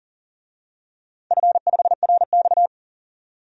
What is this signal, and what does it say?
Morse code, heard as a keyed tone, about 40 wpm, 700 Hz. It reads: F5RX
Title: DXCC-100 Call Signs - 4 Characters - 40wpm